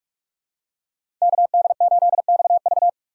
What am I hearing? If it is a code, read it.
Morse code, heard as a keyed tone, about 46 wpm, 700 Hz. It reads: KD8XV
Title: USA Call Signs - 5 Characters - 6x Character Spacing - 45wpm